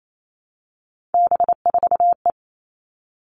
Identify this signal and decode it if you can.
Morse code, heard as a keyed tone, about 28 wpm, 700 Hz. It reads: B4E
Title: All Cut Numbers - 3-Digit 28wpm